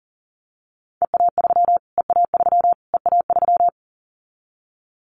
Morse code, heard as a keyed tone, about 40 wpm, 700 Hz. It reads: EA3 EA3 EA3
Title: Callsign Prefixes for Top 25 DXCC - Repeated 3x - 40wpm